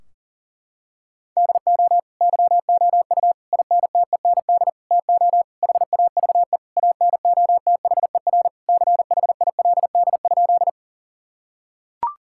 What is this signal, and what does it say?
Morse code, heard as a keyed tone, about 40 wpm, 700 Hz. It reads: DO YOU INTEND TO HAVE ANOTHER CHILD?
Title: Sentences from Top 1600 Words 40wpm